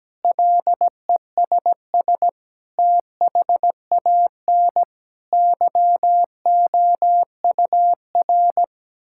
Morse code, heard as a keyed tone, about 17 wpm, 700 Hz. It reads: LESS THAN YOUR
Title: Sentences from Top 1000 Words - Review Part 5 - 17wpm